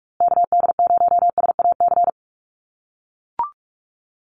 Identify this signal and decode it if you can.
Morse code, heard as a keyed tone, about 44 wpm, 700 Hz. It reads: KD0SAC
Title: USA Call Signs Repeated 2x 45wpm